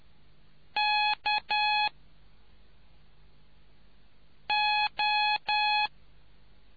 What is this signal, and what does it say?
Morse code, heard as a keyed tone, about 10 wpm, 840 Hz. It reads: K O